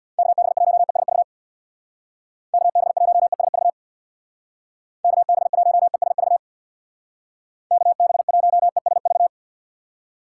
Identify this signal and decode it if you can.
Morse code, heard as a keyed tone, about 62 wpm, 700 Hz. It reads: KB1ESV KB1ESV KB1ESV KB1ESV
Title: Speed Racing - USA 2x3 Call Signs 50wpm